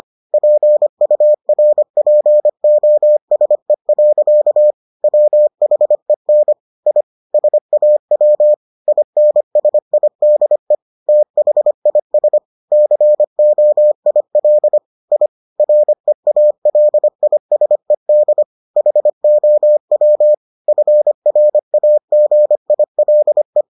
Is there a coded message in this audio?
Yes: Morse code from 0.3 to 23.6 s, heard as a keyed tone, about 25 wpm, 600 Hz. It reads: PURPOSE. WHEN I SAW INSIDE THIS COIL I REALISED HOW FRAGILE